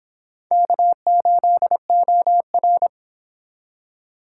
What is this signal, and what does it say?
Morse code, heard as a keyed tone, about 26 wpm, 700 Hz. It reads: K8OR
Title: Speed Racing - USA 1x2 Call Signs 22wpm